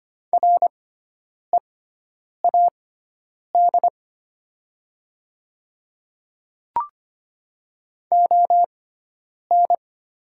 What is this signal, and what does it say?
Morse code, heard as a keyed tone, about 25 wpm, spaced as for 9 wpm, 700 Hz. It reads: READ ON
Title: 6x Character Spacing - Top 500 Words - 25wpm